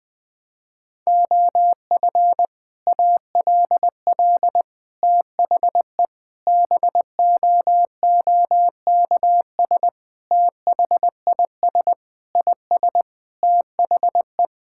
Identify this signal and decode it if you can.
Morse code, heard as a keyed tone, about 20 wpm, 700 Hz. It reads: OF ALL THE BOOKS THIS IS THE
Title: Sentences with 3-Word Phrases - Spoken Beginning - 20wpm